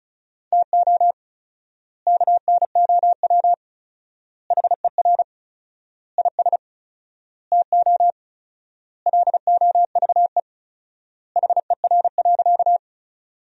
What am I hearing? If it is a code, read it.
Morse code, heard as a keyed tone, about 34 wpm, 700 Hz. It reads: TO KNOW HER IS TO LOVE HER.
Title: 4x Word-Spacing - Sentences from Top 400 Words - 35wpm